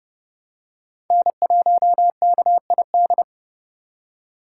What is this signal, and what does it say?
Morse code, heard as a keyed tone, about 30 wpm, 700 Hz. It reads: N1KID
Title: USA Call Signs - 8x Character Spacing - 30wpm